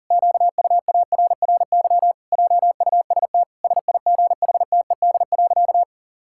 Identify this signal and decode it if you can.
Morse code, heard as a keyed tone, about 40 wpm, 700 Hz. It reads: QUARRY JUST SIGHTED.
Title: Pangram Challenge v2 - 40wpm